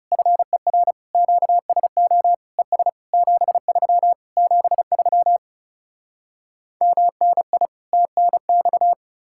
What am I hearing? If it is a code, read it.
Morse code, heard as a keyed tone, about 37 wpm, 700 Hz. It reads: FER QSO ES 73 73 MNI TNX
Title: Speed-Racing Common QSO Phrases - 30wpm